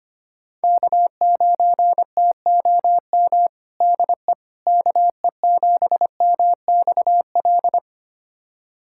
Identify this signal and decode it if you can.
Morse code, heard as a keyed tone, about 25 wpm, 700 Hz. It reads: K9TOM DE KE7MXL
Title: USA Callsign DE Callsign 25wpm